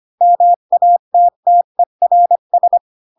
Morse code, heard as a keyed tone, about 25 wpm, 700 Hz. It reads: MATTERS
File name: Pinocchio / Pinocchio ch3 20 25 ik1hge